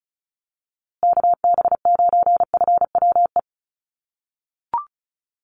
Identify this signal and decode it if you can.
Morse code, heard as a keyed tone, about 34 wpm, 700 Hz. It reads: KB9FWE
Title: USA Call Signs Repeated 2x 35wpm